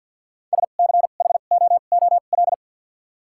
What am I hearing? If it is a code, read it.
Morse code, heard as a keyed tone, about 53 wpm, spaced as for 37 wpm, 700 Hz. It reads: SCHOOL
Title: 2x Character Spacing - Top 300 Words - 50wpm